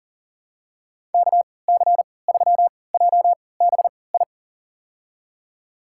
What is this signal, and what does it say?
Morse code, heard as a keyed tone, about 40 wpm, spaced as for 23 wpm, 700 Hz. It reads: KC3JBI
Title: USA Call Signs - 6 Characters - 3x Character Spacing - 40wpm